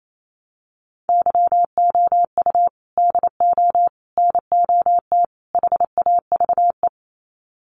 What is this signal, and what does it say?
Morse code, heard as a keyed tone, about 28 wpm, 700 Hz. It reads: YOU DO NOT HAVE
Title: Sets of 4 Words 28wpm